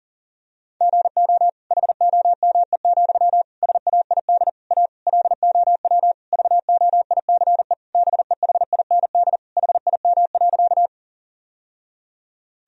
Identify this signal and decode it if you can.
Morse code, heard as a keyed tone, about 40 wpm, 700 Hz. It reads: GO HOME, SAID A LOW VOICE BEHIND HIM.
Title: Sentences from Top 1000 Words - Review Part 1 - 40wpm